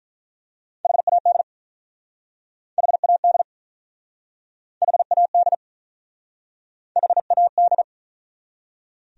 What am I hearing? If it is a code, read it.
Morse code, heard as a keyed tone, about 43 wpm, 700 Hz. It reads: HAD HAD HAD HAD
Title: Speed-Racing Top 200 Words 35wpm